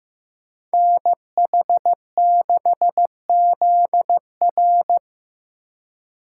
Morse code, heard as a keyed tone, about 15 wpm, 700 Hz. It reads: NH6ZR
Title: USA Call Signs - 5 Characters - 4x Character Spacing - 15wpm